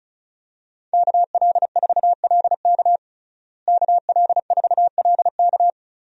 Morse code, heard as a keyed tone, about 36 wpm, 700 Hz. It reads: KL4LK KL4LK
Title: USA Call Signs - 5 Characters - Repeated 3x - 35wpm